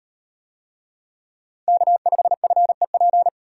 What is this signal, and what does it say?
Morse code, heard as a keyed tone, about 38 wpm, 700 Hz. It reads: K5FEP